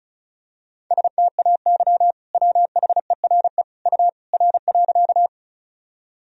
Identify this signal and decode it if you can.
Morse code, heard as a keyed tone, about 36 wpm, 700 Hz. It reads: STAY WHERE U R.
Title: Sentences from Top 500 Words and Abbreviations 35wpm